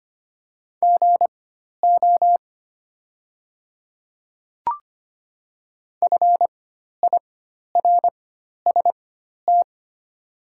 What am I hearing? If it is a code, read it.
Morse code, heard as a keyed tone, about 25 wpm, spaced as for 12 wpm, 700 Hz. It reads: GO FIRST
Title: 4x Character Spacing - Top 100 Words - 25wpm